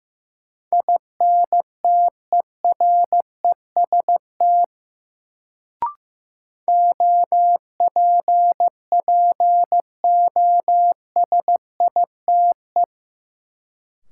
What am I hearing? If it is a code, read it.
Morse code, heard as a keyed tone, about 15 wpm, 700 Hz. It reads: INTEREST OPPOSITE
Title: Sets of 5 Words Encore - New Word Drill - 15wpm